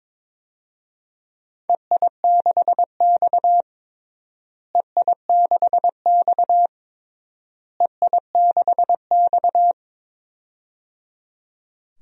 Morse code, heard as a keyed tone, about 22 wpm, 700 Hz. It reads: EI6X EI6X EI6X
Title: DXCC-100 Call Signs - 4 Characters - Repeated 3x - 22wpm